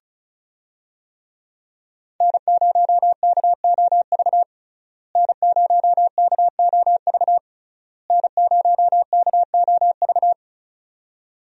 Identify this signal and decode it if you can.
Morse code, heard as a keyed tone, about 36 wpm, 700 Hz. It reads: N0KOV N0KOV N0KOV
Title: USA Call Signs - 5 Characters - Repeated 3x - 35wpm